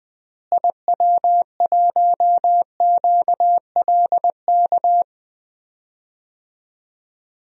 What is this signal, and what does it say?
Morse code, heard as a keyed tone, about 20 wpm, 700 Hz. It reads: IW1QLK
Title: DXCC-100 Call Signs - 6 Characters - Repeated 2x - 20wpm